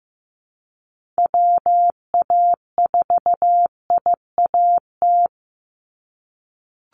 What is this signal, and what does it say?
Morse code, heard as a keyed tone, about 15 wpm, 700 Hz. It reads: WA4IAT
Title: USA Call Signs 15wpm